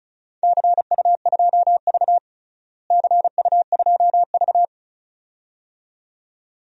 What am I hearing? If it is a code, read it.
Morse code, heard as a keyed tone, about 36 wpm, 700 Hz. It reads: CU2V CU2V
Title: DXCC-100 Call Signs - 4 Characters - Repeated 2x - 35wpm